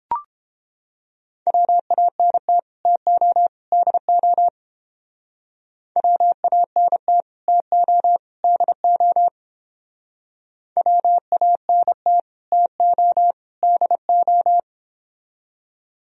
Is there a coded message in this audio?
Yes: Morse code from 1.5 to 14.6 s, heard as a keyed tone, about 29 wpm, 700 Hz. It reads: WANT TO DO WANT TO DO WANT TO DO